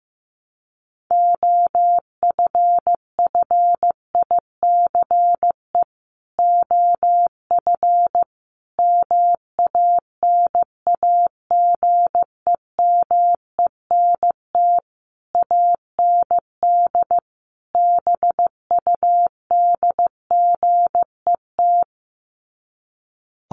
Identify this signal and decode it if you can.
Morse code, heard as a keyed tone, about 15 wpm, 700 Hz. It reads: OFFICE OF MANAGEMENT AND BUDGET
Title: Sets of 5 Words Encore2 15wpm